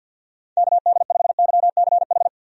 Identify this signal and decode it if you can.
Morse code, heard as a keyed tone, about 50 wpm, 700 Hz. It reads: KD5YCH
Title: USA Call Signs - 6 Characters - Repeated 3x - 50wpm